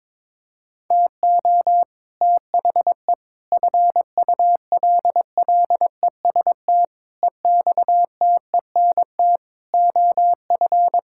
Morse code, heard as a keyed tone, about 22 wpm, 700 Hz. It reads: TO THE FULLEST EXTENT OF
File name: practice / 22wpm / Sets of 7 Words - 22wpm